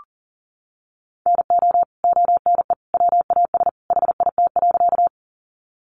Morse code, heard as a keyed tone, about 40 wpm, 700 Hz. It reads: NO ONE WAS HIT.
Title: Sentences from Top 1200 Words 40wpm